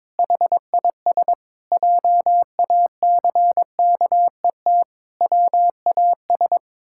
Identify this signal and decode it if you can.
Morse code, heard as a keyed tone, about 22 wpm, 700 Hz. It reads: HIS JACKET WAS
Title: Pangram Challenge v2 - 22wpm